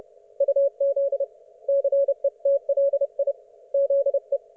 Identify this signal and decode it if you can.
Morse code, heard as a keyed tone, about 30 wpm, 560 Hz. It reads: UZ CETLI ZE
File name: Archiv / CUC179 / ok1fhi cul179